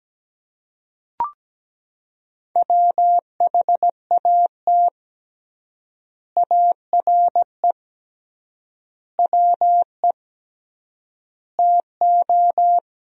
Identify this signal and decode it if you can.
Morse code, heard as a keyed tone, about 17 wpm, 700 Hz. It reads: WHAT ARE WE TO
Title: 3x Word-Spacing - Sentences from Top 100 Words - 17wpm